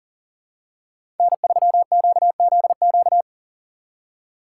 Morse code, heard as a keyed tone, about 40 wpm, 700 Hz. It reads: N3QZQ
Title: USA Call Signs - 5 Characters - 2x Character Spacing - 40wpm